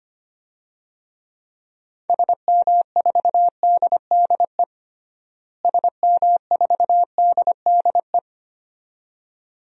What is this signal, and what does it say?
Morse code, heard as a keyed tone, about 25 wpm, 700 Hz. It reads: SM4DDE SM4DDE
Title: DXCC-100 Call Signs - 6 Characters - Repeated 2x - 25wpm